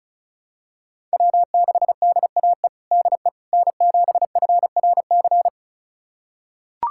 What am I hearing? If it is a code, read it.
Morse code, heard as a keyed tone, about 36 wpm, 700 Hz. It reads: W6DAE DE N7FRC
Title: USA Callsign DE Callsign - 5 Characters - 35wpm